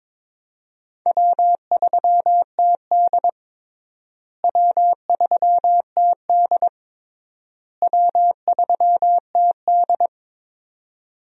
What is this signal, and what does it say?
Morse code, heard as a keyed tone, about 22 wpm, 700 Hz. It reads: W3TD W3TD W3TD
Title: USA Call Signs - 4 Characters - Repeated 3x - 22wpm